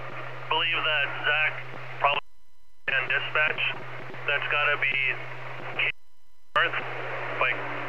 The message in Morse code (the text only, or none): none